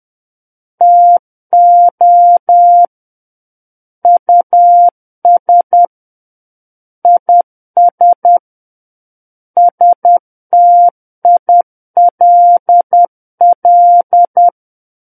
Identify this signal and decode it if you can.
Morse code, heard as a keyed tone, about 10 wpm, 700 Hz. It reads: TO US IS STILL
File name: Book One / 01-Chapter 1 - 10 WPM